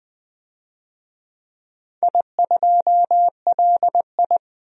Morse code, heard as a keyed tone, about 20 wpm, 700 Hz. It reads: I2LI
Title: DXCC-100 Call Signs - 4 Characters - 20wpm